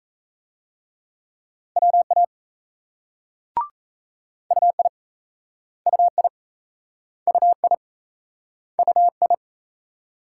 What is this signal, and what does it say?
Morse code, heard as a keyed tone, about 37 wpm, 700 Hz. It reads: WA UI UI UI UI